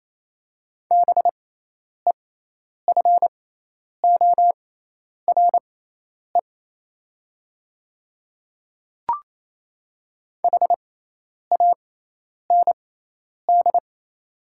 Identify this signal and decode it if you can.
Morse code, heard as a keyed tone, about 28 wpm, spaced as for 10 wpm, 700 Hz. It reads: BEFORE HAND
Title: 6x Character Spacing - Top 300 Words - 28wpm